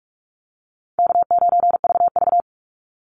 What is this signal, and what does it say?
Morse code, heard as a keyed tone, about 44 wpm, 700 Hz. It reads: K9VV
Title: Call Signs - 45wpm